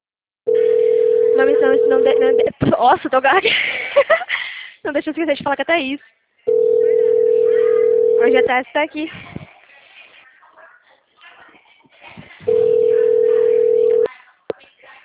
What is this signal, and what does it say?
Ringback tone (North America), 440 + 480 Hz, with speech.
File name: out- -2001- - - .4